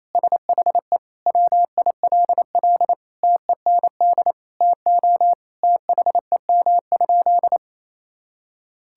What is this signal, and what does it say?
Morse code, heard as a keyed tone, about 28 wpm, 700 Hz. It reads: SHE WILL TEND TO THEM?
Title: Sentences from Top 1000 Words - Review Part 5 - 28wpm